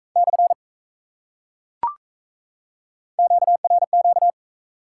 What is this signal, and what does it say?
Morse code, heard as a keyed tone, about 35 wpm, 700 Hz. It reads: / QRQ
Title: Speed-Racing Top 180 QSO Elements 28wpm